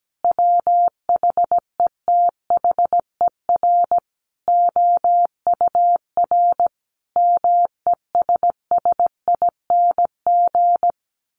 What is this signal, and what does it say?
Morse code, heard as a keyed tone, about 17 wpm, 700 Hz. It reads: WHETHER OUR MESSING
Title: A Sound Of Thunder 17wpm